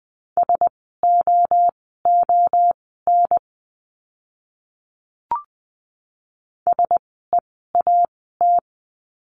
Morse code, heard as a keyed tone, about 20 wpm, spaced as for 14 wpm, 700 Hz. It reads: SOON SEAT
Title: 061 Lesson 11f - Letter h - Words 20wpm